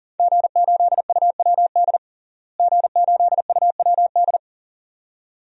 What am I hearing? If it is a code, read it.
Morse code, heard as a keyed tone, about 40 wpm, 700 Hz. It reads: G8UWD G8UWD
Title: DXCC-100 Call Signs - 5 Characters - Repeated 2x - 40wpm